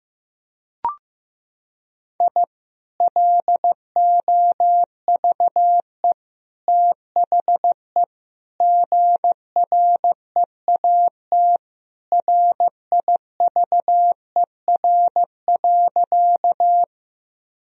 Morse code, heard as a keyed tone, about 15 wpm, 700 Hz. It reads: I LOVE THE GREAT RIVER.
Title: Sentences from Top 1000 Words - Review Part 1 - 15wpm